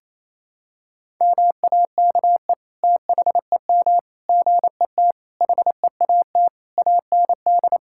Morse code, heard as a keyed tone, about 28 wpm, 700 Hz. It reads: MAKE THEM GET HEAT AND
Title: Sentences from Top 1600 Words 28wpm